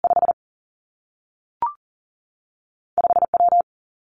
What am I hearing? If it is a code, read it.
Morse code, heard as a keyed tone, about 39 wpm, 700 Hz. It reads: 5 5 W
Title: Two Challenging Letter-Number Combinations 40wpm